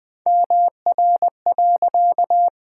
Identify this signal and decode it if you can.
Morse code, heard as a keyed tone, about 20 wpm, 700 Hz. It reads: MR.
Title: Pangram Challenge v2 - 20wpm